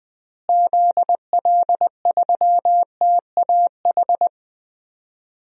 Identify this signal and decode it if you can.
Morse code, heard as a keyed tone, about 21 wpm, 700 Hz. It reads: ZL3TAH
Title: DXCC-100 Call Signs - 6 Characters - Repeated 3x - 20wpm